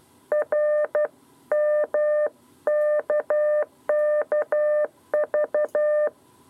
Morse code, heard as a keyed tone, about 12 wpm, 600 Hz. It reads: RMKKV
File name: 7b lezione cw 2